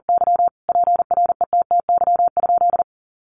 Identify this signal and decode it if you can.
Morse code, heard as a keyed tone, about 40 wpm, 700 Hz. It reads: Y PRETTY?
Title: Sentences from Top 1000 Words 40wpm